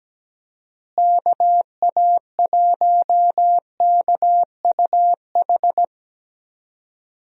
Morse code, heard as a keyed tone, about 17 wpm, 700 Hz. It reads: KA1KUH
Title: USA Call Signs - 6 Characters - Repeated 3x - 17wpm